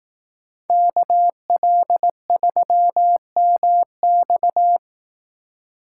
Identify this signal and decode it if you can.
Morse code, heard as a keyed tone, about 18 wpm, 700 Hz. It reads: KL3MX
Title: Speed Racing - USA 2x2 Call Signs 15wpm